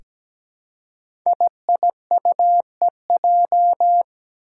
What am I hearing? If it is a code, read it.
Morse code, heard as a keyed tone, about 17 wpm, 700 Hz. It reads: IIUEJ
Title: Five Challenging Letter-Number Combinations 17wpm